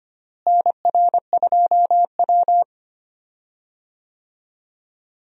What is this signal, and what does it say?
Morse code, heard as a keyed tone, about 25 wpm, 700 Hz. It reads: NR2W
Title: USA Call Signs - 4 Characters - Repeated 3x - 25wpm